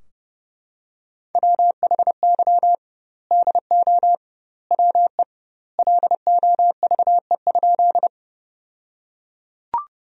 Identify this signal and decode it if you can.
Morse code, heard as a keyed tone, about 30 wpm, 700 Hz. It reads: WHY DO WE LOVE?
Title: 2x Word-Spacing - Sentences from Top 400 Words - 30wpm